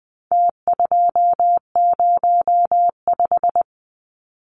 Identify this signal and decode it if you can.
Morse code, heard as a keyed tone, about 20 wpm, 700 Hz. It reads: T205
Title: Numbers - Contest Style - 4-Digit 20wpm